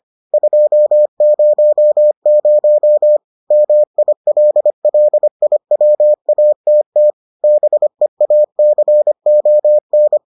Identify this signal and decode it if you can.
Morse code, heard as a keyed tone, about 25 wpm, 600 Hz. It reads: 200 MILLIWATT BEACON